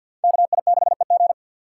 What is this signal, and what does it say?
Morse code, heard as a keyed tone, about 47 wpm, 700 Hz. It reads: KI6EG
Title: USA Call Signs - 5 Characters - 6x Character Spacing - 50wpm